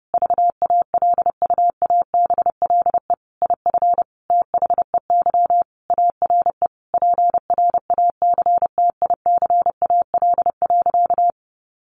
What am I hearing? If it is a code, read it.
Morse code, heard as a keyed tone, about 30 wpm, 700 Hz. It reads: VALUABLE IF THEY ARE PRACTICAL.
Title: Sentences from Top 1400 Words 30wpm